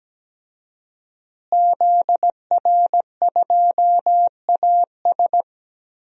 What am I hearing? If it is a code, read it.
Morse code, heard as a keyed tone, about 17 wpm, 700 Hz. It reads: ZR2AS
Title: DXCC-100 Call Signs - 5 Characters - Repeated 3x - 17wpm